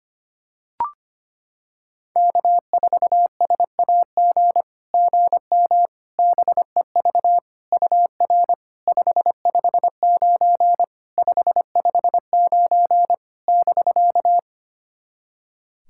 Morse code, heard as a keyed tone, about 25 wpm, 700 Hz. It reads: K4SAG GM BEV UR 559 559 <BK>
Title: SOTA QSO Practice 25wpm